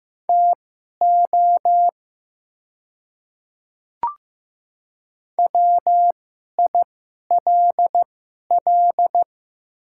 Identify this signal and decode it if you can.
Morse code, heard as a keyed tone, about 15 wpm, spaced as for 11 wpm, 700 Hz. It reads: TO WILL
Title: 2x Character Spacing - Top 100 Words - 15wpm